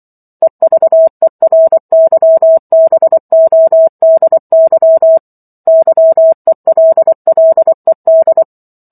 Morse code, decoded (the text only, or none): EVERYBODY YELLED